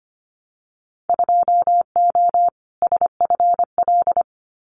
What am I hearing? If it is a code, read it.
Morse code, heard as a keyed tone, about 25 wpm, 700 Hz. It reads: 2O SFL
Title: Winter Field Day Exchanges 25wpm